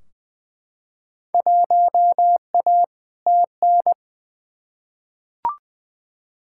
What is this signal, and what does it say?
Morse code, heard as a keyed tone, about 20 wpm, 700 Hz. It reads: 1A TN
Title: Field Day Exchanges - 20wpm